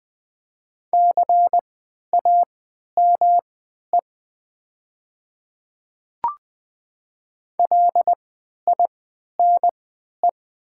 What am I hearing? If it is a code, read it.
Morse code, heard as a keyed tone, about 20 wpm, spaced as for 11 wpm, 700 Hz. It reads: CAME LINE
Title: 3x Character Spacing - Top 300 Words - 20wpm